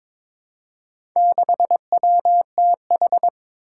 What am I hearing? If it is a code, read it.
Morse code, heard as a keyed tone, about 22 wpm, 700 Hz. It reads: 6WTH
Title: Four Challenging Letter-Number Combinations 22wpm